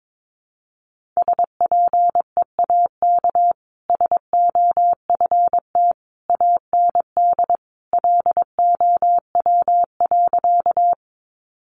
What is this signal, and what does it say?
Morse code, heard as a keyed tone, about 22 wpm, 700 Hz. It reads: SPEAK SOFT AND LOW.
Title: Sentences from Top 1500 Words 22wpm